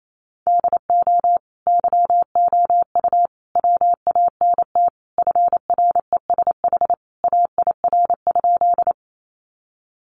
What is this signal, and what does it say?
Morse code, heard as a keyed tone, about 28 wpm, 700 Hz. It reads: DO YOU WANT FRESH AIR?
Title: Sentences from Top 1200 Words 28wpm